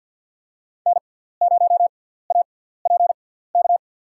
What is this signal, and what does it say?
Morse code, heard as a keyed tone, about 52 wpm, spaced as for 17 wpm, 700 Hz. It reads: N0APK
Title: USA Call Signs - 5 Characters - 6x Character Spacing - 50wpm